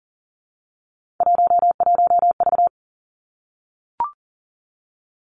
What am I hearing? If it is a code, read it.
Morse code, heard as a keyed tone, about 38 wpm, 700 Hz. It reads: 11V
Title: Three Challenging Letter-Number Combinations 40wpm